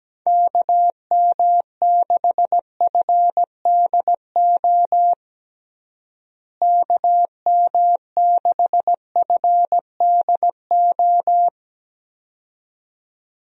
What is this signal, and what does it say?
Morse code, heard as a keyed tone, about 17 wpm, 700 Hz. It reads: KM6FDO KM6FDO
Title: USA Call Signs - 6 Characters - Repeated 2x - 17wpm